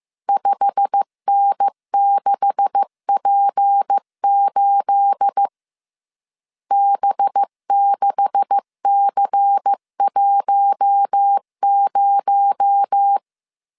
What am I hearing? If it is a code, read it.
Morse code, heard as a keyed tone, about 15 wpm, 800 Hz. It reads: 5N6P8 B6C10
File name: CW / DET / 1 9 1500C S A 14wpm 3222